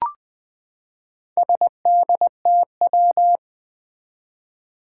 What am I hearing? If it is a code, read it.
Morse code, heard as a keyed tone, about 21 wpm, 700 Hz. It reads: SDTW
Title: Four Challenging Letter-Number Combinations 20wpm